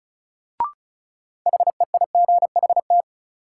Morse code, heard as a keyed tone, about 36 wpm, 700 Hz. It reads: HEIGHT
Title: Sentences from Top 1600 Words - New Word Drill - 35wpm